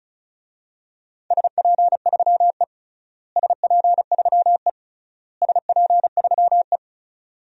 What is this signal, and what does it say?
Morse code, heard as a keyed tone, about 36 wpm, 700 Hz. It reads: SP3E SP3E SP3E
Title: DXCC-100 Call Signs - 4 Characters - Repeated 3x - 35wpm